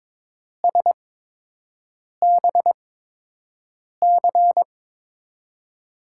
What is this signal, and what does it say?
Morse code, heard as a keyed tone, about 22 wpm, spaced as for 6 wpm, 700 Hz. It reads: SBC
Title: USA Call Signs - 8x Character Spacing - 22wpm